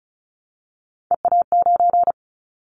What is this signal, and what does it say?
Morse code, heard as a keyed tone, about 36 wpm, 700 Hz. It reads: EA9
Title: Callsign Prefixes for Top 100 DXCC - 35wpm